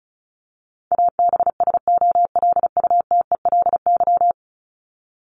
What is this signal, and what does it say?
Morse code, heard as a keyed tone, about 34 wpm, 700 Hz. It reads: ABSOLUTELY
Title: LY Suffix - Top 100 Words with Base Words 35wpm